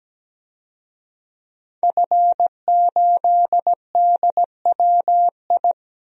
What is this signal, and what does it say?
Morse code, heard as a keyed tone, about 17 wpm, 700 Hz. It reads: F8DWI
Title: DXCC-100 Call Signs - 5 Characters - Repeated 3x - 17wpm